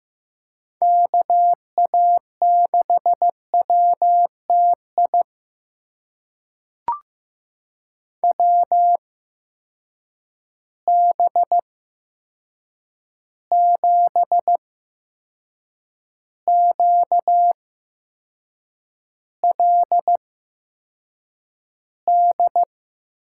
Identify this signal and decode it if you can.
Morse code, heard as a keyed tone, about 15 wpm, spaced as for 6 wpm, 700 Hz. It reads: KA6WTI WB7QLD
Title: USA Call Signs - 6 Characters - 8x Character Spacing - 15wpm